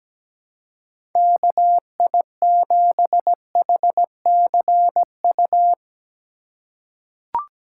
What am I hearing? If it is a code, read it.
Morse code, heard as a keyed tone, about 17 wpm, 700 Hz. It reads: KI7HCU
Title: USA Call Signs - 6 Characters - 2x Character Spacing - 17wpm